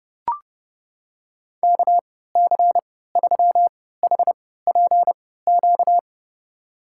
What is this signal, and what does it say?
Morse code, heard as a keyed tone, about 30 wpm, spaced as for 17 wpm, 700 Hz. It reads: KC3HPQ
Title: USA Call Signs - 6 Characters - 3x Character Spacing - 30wpm